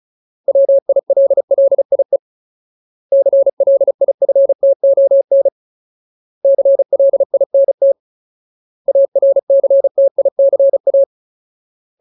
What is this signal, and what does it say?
Morse code, heard as a keyed tone, about 34 wpm, 550 Hz. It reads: WILLIE CLIFTON CLINT ARCTICA
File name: Ultimate Word List / 35WPM Ultimate